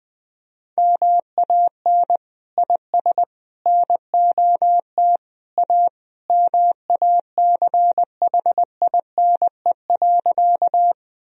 Morse code, heard as a keyed tone, about 20 wpm, 700 Hz. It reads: MAN IS NOT A MACHINE.
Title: Sentences from Top 1000 Words - Review Part 2 - 20wpm